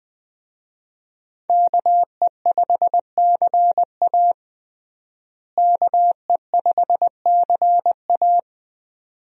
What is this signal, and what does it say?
Morse code, heard as a keyed tone, about 21 wpm, 700 Hz. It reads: KE5CA KE5CA
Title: USA Call Signs - 5 Characters - Repeated 2x - 20wpm